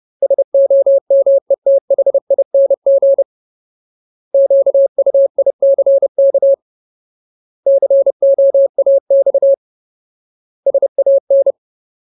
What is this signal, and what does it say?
Morse code, heard as a keyed tone, about 30 wpm, 550 Hz. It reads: SOMETHING QUICK COAX SAN
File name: Ultimate Word List / 30WPM Ultimate